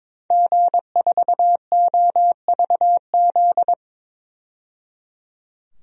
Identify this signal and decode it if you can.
Morse code, heard as a keyed tone, about 22 wpm, 700 Hz. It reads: G4OVZ
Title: DXCC-100 Call Signs - 5 Characters - 22wpm